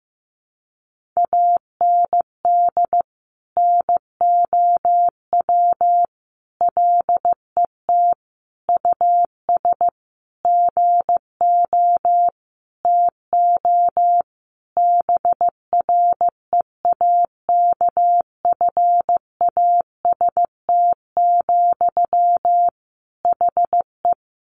Morse code, heard as a keyed tone, about 15 wpm, 700 Hz. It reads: AND NOW LET US GO TO BREAKFAST, HE
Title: Sentences from Top 1500 Words 15wpm